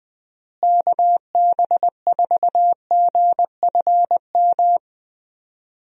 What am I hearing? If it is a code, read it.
Morse code, heard as a keyed tone, about 21 wpm, 700 Hz. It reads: KB4GFM
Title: USA Call Signs - 6 Characters - Repeated 2x - 20wpm